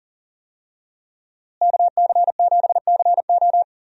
Morse code, heard as a keyed tone, about 37 wpm, 700 Hz. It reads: KC7CO
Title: USA Call Signs - 5 Characters - Repeated 2x - 40wpm